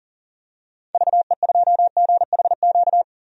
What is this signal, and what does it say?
Morse code, heard as a keyed tone, about 40 wpm, 700 Hz. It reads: VE2GHQ
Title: DXCC-100 Call Signs - 6 Characters - 40wpm